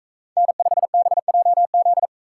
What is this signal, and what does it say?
Morse code, heard as a keyed tone, about 41 wpm, 700 Hz. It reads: N5BJZ